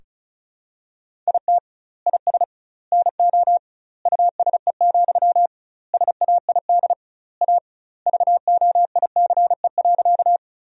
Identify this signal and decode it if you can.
Morse code, heard as a keyed tone, about 34 wpm, 700 Hz. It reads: IT IS NO USE, SAID A VOICE.
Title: 2x Word-Spacing - Sentences from Top 500 Words - 35wpm